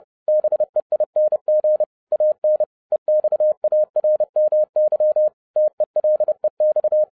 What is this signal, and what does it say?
Morse code, heard as a keyed tone, about 30 wpm, 610 Hz. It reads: BEING AN EXARMY TELEX